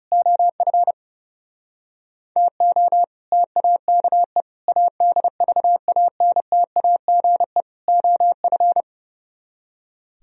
Morse code, heard as a keyed tone, about 28 wpm, 700 Hz. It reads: OF TO TAKE ADVANTAGE OF
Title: Speed-Racing - Sets of 4 Words 30wpm